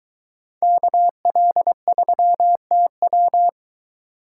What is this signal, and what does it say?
Morse code, heard as a keyed tone, about 24 wpm, 700 Hz. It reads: KL3TW